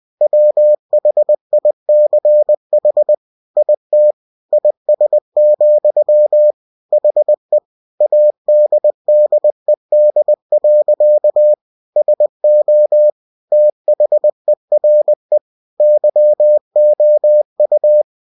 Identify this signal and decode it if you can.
Morse code, heard as a keyed tone, about 20 wpm, 600 Hz. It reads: WHICH IT IS, HE ADDED. SO THERE YOU
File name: cw-books / winnie-the-pooh / winnie-20wpm-page-0028